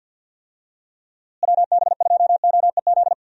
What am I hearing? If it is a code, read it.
Morse code, heard as a keyed tone, about 50 wpm, 700 Hz. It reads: WB2OEZ
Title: USA Call Signs - 6 Characters - Repeated 2x - 50wpm